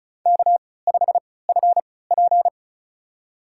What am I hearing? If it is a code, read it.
Morse code, heard as a keyed tone, about 36 wpm, spaced as for 20 wpm, 700 Hz. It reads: K5FP
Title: USA Call Signs - 3x Character Spacing - 35wpm